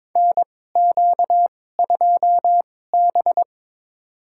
Morse code, heard as a keyed tone, about 22 wpm, spaced as for 16 wpm, 700 Hz. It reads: NQ2B
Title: USA Call Signs - 4 Characters - 2x Character Spacing - 22wpm